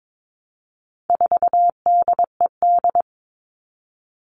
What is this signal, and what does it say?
Morse code, heard as a keyed tone, about 22 wpm, 700 Hz. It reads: 4DED
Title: All Cut Numbers - 4-Digit 22wpm